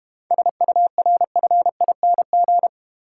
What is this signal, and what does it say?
Morse code, heard as a keyed tone, about 32 wpm, 700 Hz. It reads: SURFING